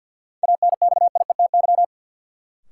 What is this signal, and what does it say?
Morse code, heard as a keyed tone, about 47 wpm, 700 Hz. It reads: ANXIETY
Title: Sentences from Top 1600 Words - New Word Drill - 50wpm